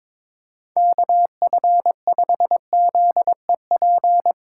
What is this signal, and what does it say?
Morse code, heard as a keyed tone, about 22 wpm, 700 Hz. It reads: KF5ZEP
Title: USA Call Signs - 6 Characters - 3x Character Spacing - 22wpm